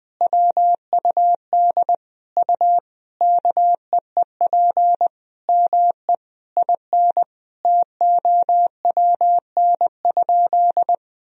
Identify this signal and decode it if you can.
Morse code, heard as a keyed tone, about 20 wpm, 700 Hz. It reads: WUD U KEEP ME IN TOWN?
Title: Sentences from Top 500 Words and Abbreviations 20wpm